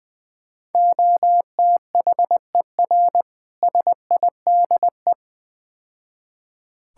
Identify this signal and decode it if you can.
Morse code, heard as a keyed tone, about 21 wpm, 700 Hz. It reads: OTHER SIDE
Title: Build A Phrase - 4-Word Phrases - 20wpm